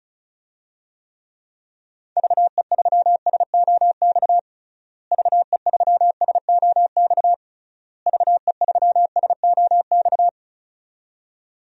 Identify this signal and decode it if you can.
Morse code, heard as a keyed tone, about 36 wpm, 700 Hz. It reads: VE3SOX VE3SOX VE3SOX
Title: DXCC-100 Call Signs - 6 Characters - Repeated 3x - 35wpm